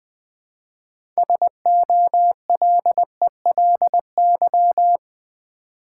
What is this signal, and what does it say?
Morse code, heard as a keyed tone, about 20 wpm, 700 Hz. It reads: SOLELY
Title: Build A Word - 2 Letters At A Time - 6-Letter Words - 20wpm